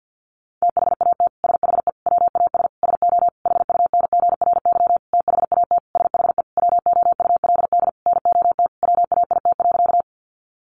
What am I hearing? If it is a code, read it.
Morse code, heard as a keyed tone, about 50 wpm, 700 Hz. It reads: THAT SHE WAS SO HUNGRY THAT SHE WOULD NOT WAIT.
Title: Sentences from Top 1400 Words 50wpm